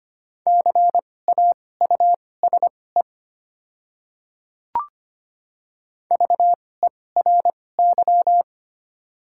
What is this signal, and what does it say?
Morse code, heard as a keyed tone, about 25 wpm, spaced as for 18 wpm, 690 Hz. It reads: CAUSE VERY
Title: 2x Character Spacing - Top 300 Words - 25wpm